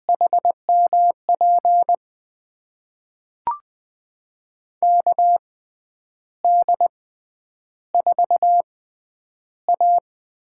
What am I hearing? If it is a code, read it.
Morse code, heard as a keyed tone, about 20 wpm, spaced as for 9 wpm, 700 Hz. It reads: HMP KD4A
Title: USA Call Signs - 6 Characters - 6x Character Spacing - 20wpm